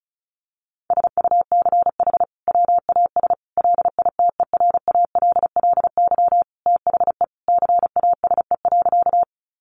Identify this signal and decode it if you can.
Morse code, heard as a keyed tone, about 34 wpm, 700 Hz. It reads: SUCH WAS LITERALLY THE CASE.
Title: Sentences from Top 1400 Words 35wpm